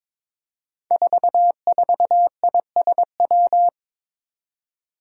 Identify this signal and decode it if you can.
Morse code, heard as a keyed tone, about 22 wpm, 700 Hz. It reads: 44ISW
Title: Five Challenging Letter-Number Combinations 22wpm